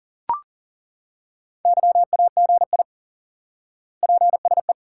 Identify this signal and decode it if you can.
Morse code, heard as a keyed tone, about 40 wpm, 700 Hz. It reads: YAGI PSE
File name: IQE Course 3 - Lesson 89 - YAGI - Two Elements - 40wpm